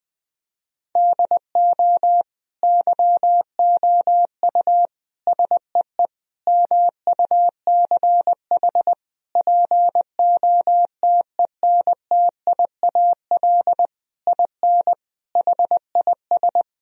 Morse code, heard as a keyed tone, about 21 wpm, 700 Hz. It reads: DO YOU SEE MUCH POTENTIAL IN HIS
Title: Sentences from Top 1000 Words - Review Part 4 - 20wpm